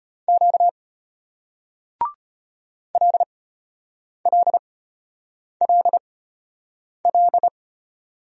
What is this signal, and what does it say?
Morse code, heard as a keyed tone, about 32 wpm, spaced as for 8 wpm, 700 Hz. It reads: Q LLLL